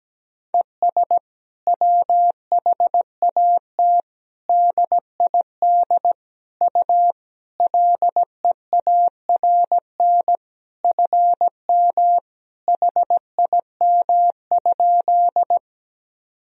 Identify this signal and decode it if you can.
Morse code, heard as a keyed tone, about 17 wpm, 700 Hz. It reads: ES WHAT DID U LEARN FM HIM?
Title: Sentences from Top 300 Words and Abbreviations 17wpm